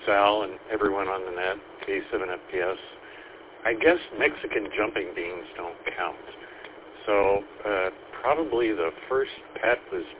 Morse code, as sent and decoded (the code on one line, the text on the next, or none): none